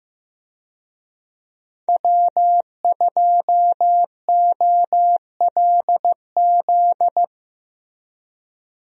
Morse code, heard as a keyed tone, about 15 wpm, 700 Hz. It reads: W2OLZ
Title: USA Call Signs - 5 Characters - Repeated 3x - 15wpm